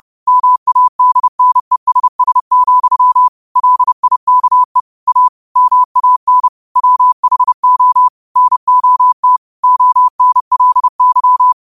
Morse code, heard as a keyed tone, about 30 wpm, 1000 Hz. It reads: MADNESS, LIKE A MAN WHO NOT ONLY
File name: qotdfiles / 30wpm